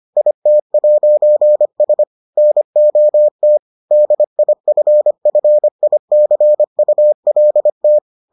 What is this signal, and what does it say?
Morse code, heard as a keyed tone, about 25 wpm, 600 Hz. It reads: IT'S NOT DIFFICULT